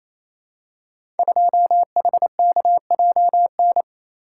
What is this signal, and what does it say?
Morse code, heard as a keyed tone, about 28 wpm, 690 Hz. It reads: 2HKJN
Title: Code Groups - 5 Letter:Numbers - 28wpm